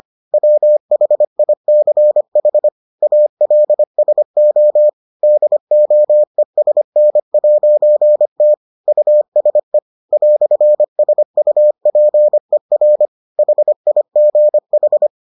Morse code, heard as a keyed tone, about 25 wpm, 600 Hz. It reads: WHICH ALSO DOESN'T USE "SUPER HIGH